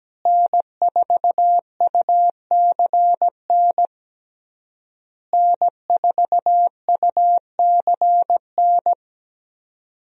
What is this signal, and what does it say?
Morse code, heard as a keyed tone, about 17 wpm, 700 Hz. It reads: N4UCN N4UCN
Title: USA Call Signs - 5 Characters - Repeated 3x - 17wpm